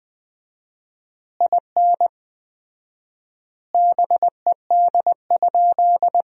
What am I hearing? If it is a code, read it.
Morse code, heard as a keyed tone, about 20 wpm, 700 Hz. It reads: IN BED?
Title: 4x Word-Spacing - Sentences from Top 500 Words - 20wpm